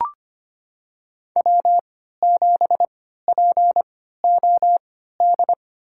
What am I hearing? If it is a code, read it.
Morse code, heard as a keyed tone, about 25 wpm, spaced as for 14 wpm, 700 Hz. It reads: W7POD
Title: USA Call Signs - 3x Character Spacing - 25wpm